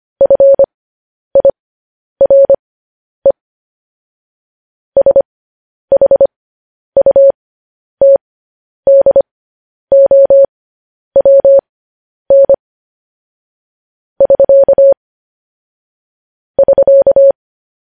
Morse code, heard as a keyed tone, about 25 wpm, spaced as for 10 wpm, 550 Hz. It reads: FIRE SHUTDOWN <SK> <SK>
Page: news headlines in morse code at 10 wpm